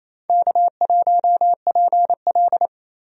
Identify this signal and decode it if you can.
Morse code, heard as a keyed tone, about 28 wpm, 700 Hz. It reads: K1PL
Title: USA Call Signs - 4 Characters - Repeated 2x - 28wpm